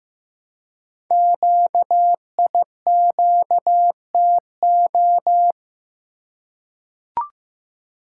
Morse code, heard as a keyed tone, about 15 wpm, 700 Hz. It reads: QIQTO
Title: Code Groups - 5 Letters - 15wpm